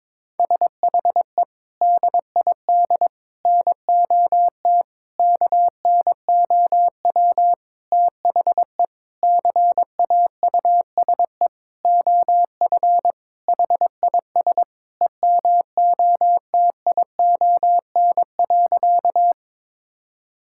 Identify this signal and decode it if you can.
Morse code, heard as a keyed tone, about 21 wpm, 700 Hz. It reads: SHE DID NOT KNOW THE CAUSE OF HIS EMOTION.
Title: Sentences from Top 1600 Words 22wpm